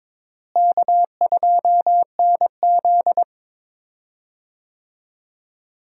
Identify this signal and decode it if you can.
Morse code, heard as a keyed tone, about 22 wpm, 700 Hz. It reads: K2NZ
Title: USA Call Signs - 4 Characters - Repeated 3x - 22wpm